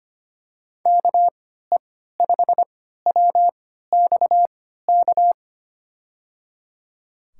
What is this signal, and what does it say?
Morse code, heard as a keyed tone, about 25 wpm, spaced as for 14 wpm, 700 Hz. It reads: KE5WXK
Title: USA Call Signs - 6 Characters - 3x Character Spacing - 25wpm